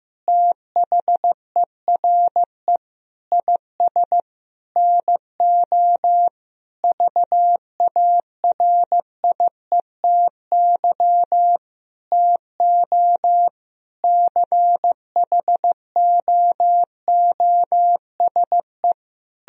Morse code, heard as a keyed tone, about 16 wpm, 700 Hz. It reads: THERE IS NO VARIETY TO CHOOSE
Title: Sentences from Top 1000 Words - Review Part 4 - 15wpm